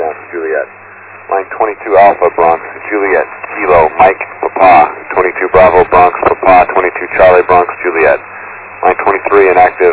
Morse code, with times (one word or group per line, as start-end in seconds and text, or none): none